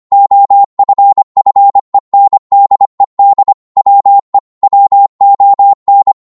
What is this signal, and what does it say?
Morse code, heard as a keyed tone, about 25 wpm, 800 Hz. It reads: OFFENDED WE WON